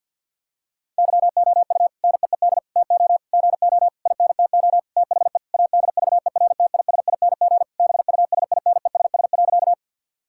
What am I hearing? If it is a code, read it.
Morse code, heard as a keyed tone, about 52 wpm, 700 Hz. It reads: YOU NEED TO GO INTO THE ADVERTISING BUSINESS.
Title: Sentences from Top 1000 Words - Review Part 6 - 50wpm